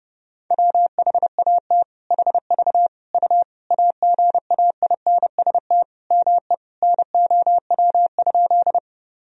Morse code, heard as a keyed tone, about 32 wpm, 700 Hz. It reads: WHAT HV U AGAINST ME NOW?
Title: Sentences from Top 400 Words and Abbreviations 30wpm